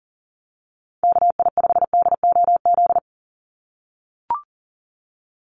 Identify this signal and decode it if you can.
Morse code, heard as a keyed tone, about 40 wpm, 700 Hz. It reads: KI5DOZ
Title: USA Call Signs 40wpm